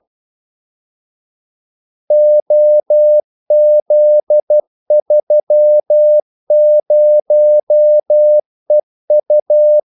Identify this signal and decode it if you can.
Morse code, heard as a keyed tone, about 12 wpm, 600 Hz. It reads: OZ30EU
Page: 2 ekstra ordmellemrum mellem kaldesignalerne